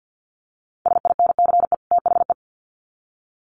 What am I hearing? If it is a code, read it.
Morse code, heard as a keyed tone, about 50 wpm, 700 Hz. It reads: SINCE THE
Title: Sets of 2 Words 50wpm